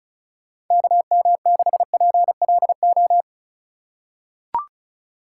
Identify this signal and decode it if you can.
Morse code, heard as a keyed tone, about 36 wpm, 700 Hz. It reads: KM6PLO
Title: USA Call Signs - 4x Character Spacing - 35wpm